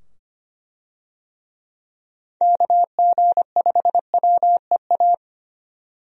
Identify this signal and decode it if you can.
Morse code, heard as a keyed tone, about 25 wpm, 700 Hz. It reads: KG5WEA
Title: USA Call Signs - 6 Characters - 25wpm